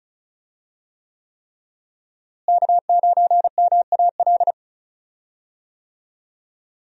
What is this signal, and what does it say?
Morse code, heard as a keyed tone, about 36 wpm, 700 Hz. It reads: K9MAL
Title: USA Call Signs - 5 Characters - Repeated 3x - 35wpm